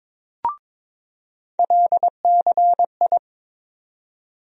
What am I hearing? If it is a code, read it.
Morse code, heard as a keyed tone, about 22 wpm, 700 Hz. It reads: LCI
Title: USA Call Sign Postfixes - 3 Letters 22wpm